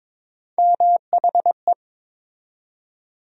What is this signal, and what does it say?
Morse code, heard as a keyed tone, about 22 wpm, 700 Hz. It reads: MHE
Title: USA Call Sign Postfixes - 3 Letters 22wpm